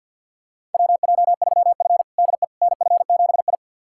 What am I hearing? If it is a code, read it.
Morse code, heard as a keyed tone, about 51 wpm, 700 Hz. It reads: WJ3F DE NF7I
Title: USA Callsign DE Callsign - 4 Characters - 50wpm